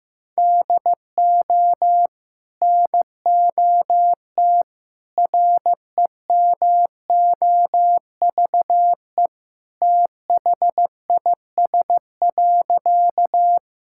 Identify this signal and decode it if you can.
Morse code, heard as a keyed tone, about 15 wpm, 700 Hz. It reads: DO NOT REMOVE THIS.
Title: Sentences from Top 1000 Words - Review Part 5 - 15wpm